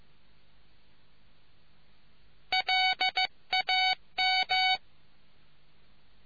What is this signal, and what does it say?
Morse code, heard as a keyed tone, about 15 wpm, 750 Hz. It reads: LAM